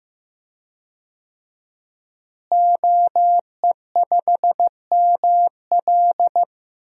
Morse code, heard as a keyed tone, about 15 wpm, 700 Hz. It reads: OE5ML
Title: DXCC-100 Call Signs - 5 Characters - Repeated 3x - 15wpm